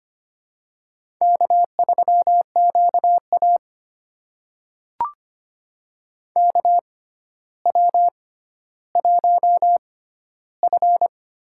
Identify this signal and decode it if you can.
Morse code, heard as a keyed tone, about 25 wpm, spaced as for 13 wpm, 700 Hz. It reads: K3QA KW1F
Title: USA Call Signs - 4 Characters - 6x Character Spacing - 25wpm